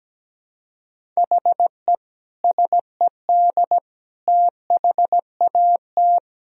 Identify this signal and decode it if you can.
Morse code, heard as a keyed tone, about 17 wpm, 700 Hz. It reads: HE SED THAT
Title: Sentences from Top 200 Words and Abbreviations 17wpm